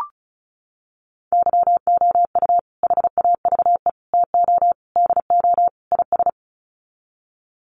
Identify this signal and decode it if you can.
Morse code, heard as a keyed tone, about 34 wpm, 700 Hz. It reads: YOU HAVE TO DO IS
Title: Sets of 5 Words 35wpm